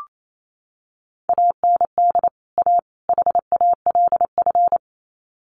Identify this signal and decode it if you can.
Morse code, heard as a keyed tone, about 28 wpm, 700 Hz. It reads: AND A HALF
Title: Sets of 3 Words 28wpm